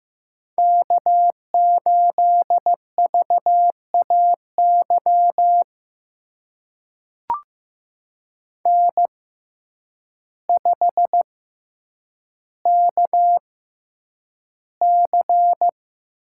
Morse code, heard as a keyed tone, about 15 wpm, spaced as for 8 wpm, 700 Hz. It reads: K8VAY N5KC
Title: USA Call Signs - 5 Characters - 6x Character Spacing - 15wpm